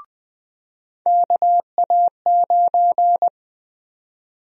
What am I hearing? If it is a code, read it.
Morse code, heard as a keyed tone, about 21 wpm, 700 Hz. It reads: KA9
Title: USA Call Sign Prefixes - 20wpm